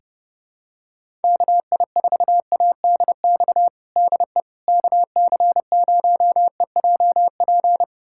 Morse code, heard as a keyed tone, about 32 wpm, 700 Hz. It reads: KI4ADX DE KC0EJP
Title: USA Callsign DE Callsign - 6 Characters - 30wpm